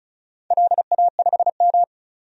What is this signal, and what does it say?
Morse code, heard as a keyed tone, about 36 wpm, 700 Hz. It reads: LA5M
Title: DXCC-100 Call Signs - 4 Characters - 35wpm